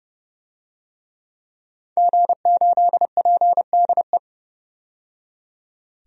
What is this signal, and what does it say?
Morse code, heard as a keyed tone, about 30 wpm, 700 Hz. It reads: G8PDE
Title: DXCC-100 Call Signs - 5 Characters - 30wpm